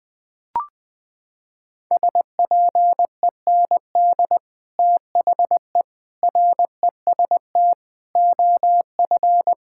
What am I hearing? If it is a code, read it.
Morse code, heard as a keyed tone, about 20 wpm, 700 Hz. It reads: SPEND THE REST OF
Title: Sets of 8 Words - 20wpm